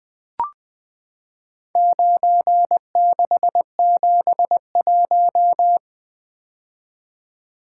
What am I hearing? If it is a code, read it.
Morse code, heard as a keyed tone, about 20 wpm, 700 Hz. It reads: 9671
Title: Four Letter-Number Combinations - 20wpm